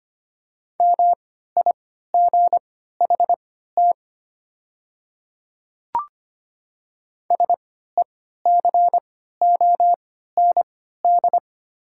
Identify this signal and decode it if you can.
Morse code, heard as a keyed tone, about 25 wpm, spaced as for 14 wpm, 700 Hz. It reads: MIGHT SECOND
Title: 3x Character Spacing - Top 300 Words - 25wpm